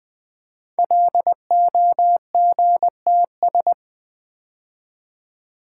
Morse code, heard as a keyed tone, about 20 wpm, 700 Hz. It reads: LOGTS
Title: Code Groups - 5 Letter:Numbers - 20wpm